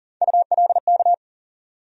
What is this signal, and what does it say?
Morse code, heard as a keyed tone, about 40 wpm, 700 Hz. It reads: ULK
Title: USA Call Sign Postfixes - 3 Letters 40wpm